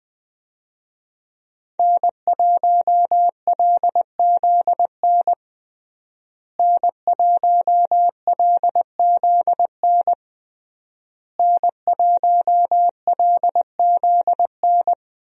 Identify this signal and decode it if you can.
Morse code, heard as a keyed tone, about 20 wpm, 700 Hz. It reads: N1LZN N1LZN N1LZN
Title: USA Call Signs - 5 Characters - Repeated 3x - 20wpm